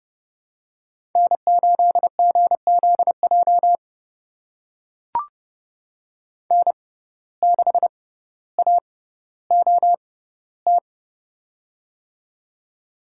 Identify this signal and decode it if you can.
Morse code, heard as a keyed tone, about 29 wpm, spaced as for 15 wpm, 700 Hz. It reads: N8GZJ N6AOT
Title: USA Call Signs - 5 Characters - 6x Character Spacing - 30wpm